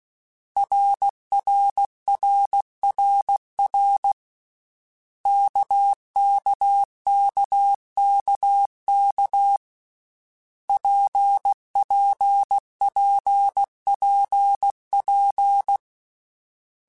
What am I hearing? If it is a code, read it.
Morse code, heard as a keyed tone, about 16 wpm, 800 Hz. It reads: RRRRR KKKKK PPPPP
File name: Basic Letter Groups By7Groups / RKPX / RKPX@13